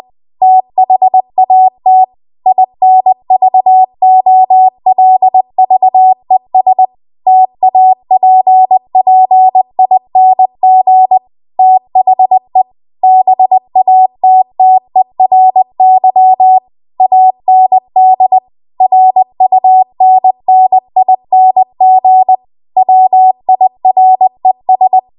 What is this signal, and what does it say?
Morse code, heard as a keyed tone, about 20 wpm, 750 Hz. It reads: THAT INVOLVES TAPPING THE BATTERY AND RUNNING WIRES
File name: Archive / 20 WPM / 20WPM